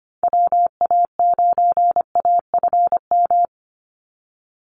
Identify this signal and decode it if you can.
Morse code, heard as a keyed tone, about 25 wpm, 700 Hz. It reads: WA9AFM
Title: USA Call Signs 25wpm